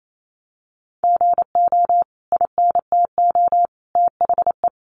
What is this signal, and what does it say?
Morse code, heard as a keyed tone, about 28 wpm, 700 Hz. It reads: GO INTO THE
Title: Sentences from Top 1500 Words 28wpm